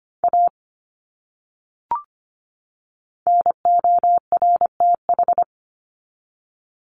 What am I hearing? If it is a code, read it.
Morse code, heard as a keyed tone, about 25 wpm, 700 Hz. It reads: A NORTH
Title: Top 300 Words 25wpm